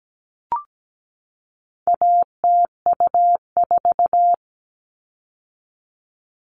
Morse code, heard as a keyed tone, about 17 wpm, 700 Hz. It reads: ATU4
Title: All Cut Numbers - 4-Digit 17wpm